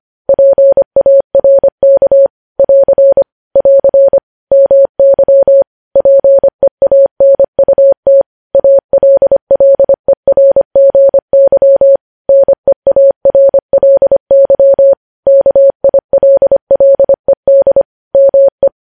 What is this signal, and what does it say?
Morse code, heard as a keyed tone, about 25 wpm, 550 Hz. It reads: PARK <AR> <AR> MY PEANUT ALLERGY NEARLY KILLED ME
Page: News Headlines in Morse Code at 25 WPM